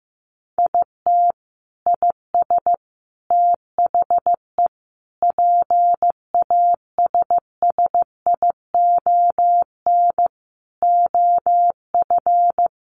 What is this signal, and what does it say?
Morse code, heard as a keyed tone, about 15 wpm, 700 Hz. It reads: IT IS THE PASSION OF
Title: Sentences from Top 1300 Words 15wpm